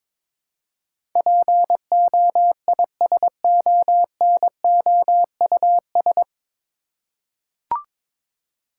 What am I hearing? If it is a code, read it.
Morse code, heard as a keyed tone, about 22 wpm, 700 Hz. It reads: POISONOUS
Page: Morse Code Ninja practice 22wpm